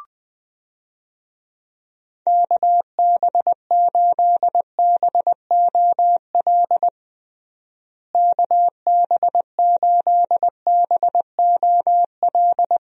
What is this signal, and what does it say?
Morse code, heard as a keyed tone, about 20 wpm, 700 Hz. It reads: KB8BOL KB8BOL
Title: USA Call Signs - 6 Characters - Repeated 2x - 20wpm